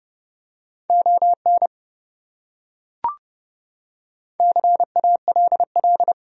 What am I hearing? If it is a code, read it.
Morse code, heard as a keyed tone, about 28 wpm, 700 Hz. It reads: ON CALL